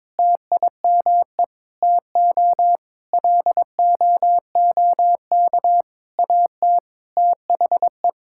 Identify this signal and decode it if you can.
Morse code, heard as a keyed tone, about 22 wpm, 700 Hz. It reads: TIME TO LOOK AT THE
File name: practice / 22wpm / Sentences with 3-Word Phrases - 22wpm